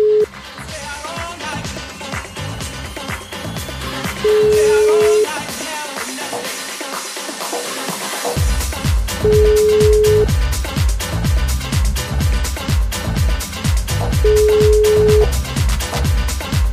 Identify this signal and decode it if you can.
Ringback tone (Europe, 425 Hz), with noise.